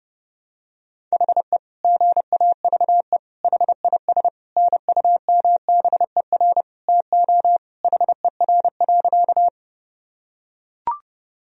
Morse code, heard as a keyed tone, about 30 wpm, 700 Hz. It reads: HE GAVE HIS NUMBER TO HER.
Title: Sentences from Top 1000 Words - Review Part 1 - 30wpm